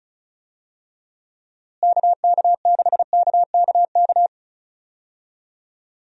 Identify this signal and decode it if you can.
Morse code, heard as a keyed tone, about 34 wpm, 700 Hz. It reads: KK6KKK
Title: USA Call Signs - 6 Characters - Repeated 3x - 35wpm